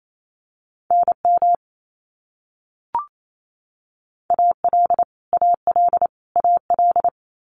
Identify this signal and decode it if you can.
Morse code, heard as a keyed tone, about 28 wpm, 700 Hz. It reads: NM AL AL AL
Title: US States Repeated 3x 28wpm